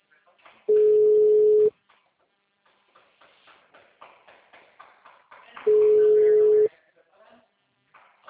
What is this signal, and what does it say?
Ringback tone (Europe, 425 Hz), with speech.